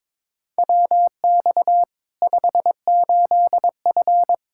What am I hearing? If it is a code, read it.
Morse code, heard as a keyed tone, about 22 wpm, 700 Hz. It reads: WX 58F
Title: Practice QSO - Exchange 2 - Info - 22wpm